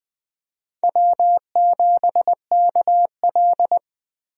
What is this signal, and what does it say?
Morse code, heard as a keyed tone, about 20 wpm, 700 Hz. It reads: W7KL
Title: USA Call Signs - 4 Characters - Repeated 2x - 20wpm